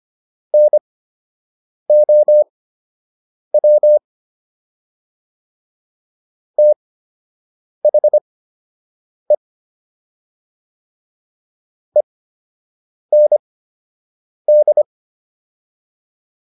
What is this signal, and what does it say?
Morse code, heard as a keyed tone, about 25 wpm, spaced as for 7 wpm, 600 Hz. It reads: NOW THE END